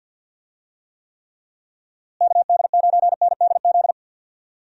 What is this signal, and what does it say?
Morse code, heard as a keyed tone, about 46 wpm, 700 Hz. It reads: KD9NDZ
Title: USA Call Signs - 6 Characters - Repeated 3x - 50wpm